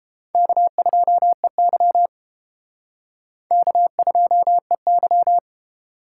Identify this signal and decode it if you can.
Morse code, heard as a keyed tone, about 29 wpm, 700 Hz. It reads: K2EY K2EY